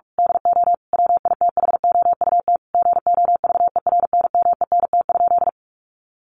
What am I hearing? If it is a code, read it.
Morse code, heard as a keyed tone, about 44 wpm, 700 Hz. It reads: DO WITHOUT GOVERNMENT?
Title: Sentences from Top 600 Words 45wpm